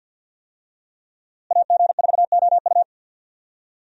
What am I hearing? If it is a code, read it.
Morse code, heard as a keyed tone, about 51 wpm, 700 Hz. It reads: AG4OU
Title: USA Call Signs - 5 Characters - Repeated 3x - 50wpm